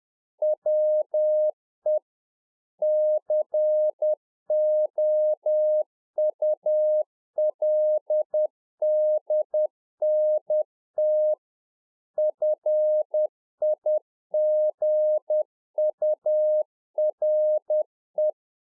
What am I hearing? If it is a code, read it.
Morse code, heard as a keyed tone, about 10 wpm, 610 Hz. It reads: WE COULDNT FIGURE